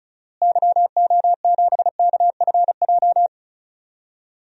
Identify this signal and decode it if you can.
Morse code, heard as a keyed tone, about 36 wpm, 700 Hz. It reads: YO7KFJ
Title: DXCC-100 Call Signs - 6 Characters - Repeated 2x - 35wpm